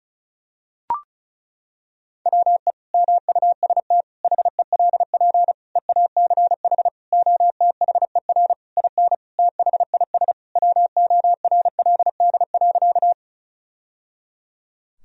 Morse code, heard as a keyed tone, about 36 wpm, 700 Hz. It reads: WE MUST HELP EACH OTHER IN THIS WORLD.
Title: Sentences from Top 1000 Words - Review Part 1 - 35wpm